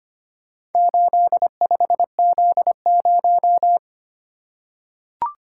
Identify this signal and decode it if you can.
Morse code, heard as a keyed tone, about 25 wpm, 700 Hz. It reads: 85Z0
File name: practice / 25wpm / Four Letter-Number Combinations - 25wpm